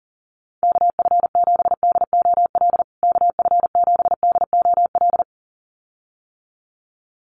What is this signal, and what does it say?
Morse code, heard as a keyed tone, about 40 wpm, 700 Hz. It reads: KF7DOL KF7DOL
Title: USA Call Signs Repeated 2x 40wpm